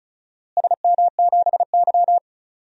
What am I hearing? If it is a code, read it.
Morse code, heard as a keyed tone, about 36 wpm, 700 Hz. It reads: SM7Y
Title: DXCC-100 Call Signs - 4 Characters - Repeated 2x - 35wpm